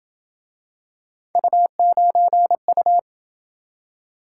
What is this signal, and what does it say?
Morse code, heard as a keyed tone, about 27 wpm, 700 Hz. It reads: U9U